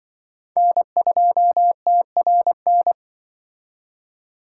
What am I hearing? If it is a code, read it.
Morse code, heard as a keyed tone, about 25 wpm, 700 Hz. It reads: N2TRN
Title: Speed Racing - USA 1x3 Call Signs 20wpm